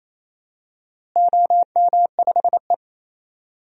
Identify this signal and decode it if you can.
Morse code, heard as a keyed tone, about 28 wpm, 700 Hz. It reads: OM5E
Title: DXCC-100 Call Signs - 4 Characters - Repeated 2x - 28wpm